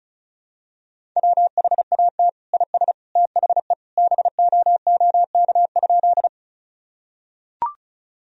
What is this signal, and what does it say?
Morse code, heard as a keyed tone, about 36 wpm, 700 Hz. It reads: WHAT IS THE BOOK?
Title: Sentences from Top 1000 Words - Review Part 1 - 35wpm